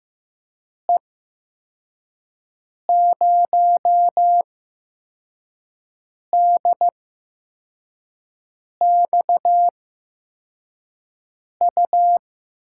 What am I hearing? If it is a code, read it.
Morse code, heard as a keyed tone, about 15 wpm, spaced as for 4 wpm, 700 Hz. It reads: E0DXU
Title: USA Call Signs - 6 Characters - 8x Character Spacing - 15wpm